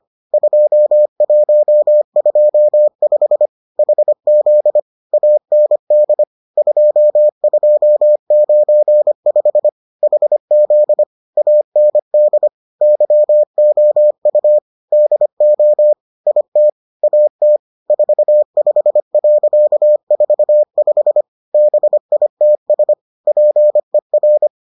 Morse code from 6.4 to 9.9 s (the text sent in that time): 2295